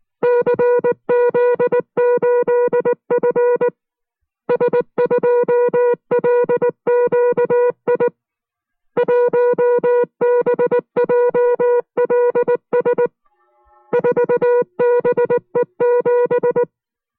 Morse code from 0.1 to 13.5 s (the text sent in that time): CZ8F S2LQI 1BJLS